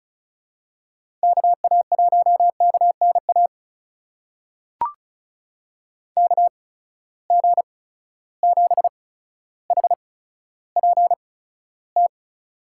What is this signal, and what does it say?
Morse code, heard as a keyed tone, about 36 wpm, spaced as for 15 wpm, 700 Hz. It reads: KA1KNA KG7HPT
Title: USA Call Signs - 6 Characters - 8x Character Spacing - 35wpm